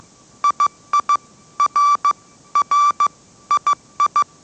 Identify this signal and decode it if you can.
Morse code, heard as a keyed tone, about 15 wpm, 1160 Hz. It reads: II R R II
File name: mrx / mrx065 16wpm